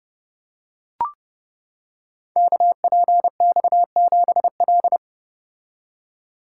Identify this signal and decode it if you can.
Morse code, heard as a keyed tone, about 30 wpm, 700 Hz. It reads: KPX7L
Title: Code Groups - 5 Letter:Numbers - 30wpm